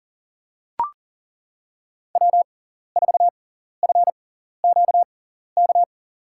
Morse code, heard as a keyed tone, about 40 wpm, spaced as for 14 wpm, 700 Hz. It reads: W4FQK
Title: USA Call Signs - 6x Character Spacing - 40wpm